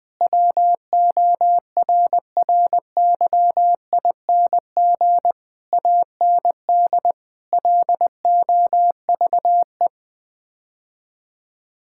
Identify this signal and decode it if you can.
Morse code, heard as a keyed tone, about 21 wpm, 700 Hz. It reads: WORRYING AND LOVE
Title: Sets of 8 Words - 20wpm